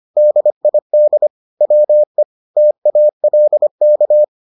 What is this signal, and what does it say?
Morse code, heard as a keyed tone, about 25 wpm, 600 Hz. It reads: DID WE TALK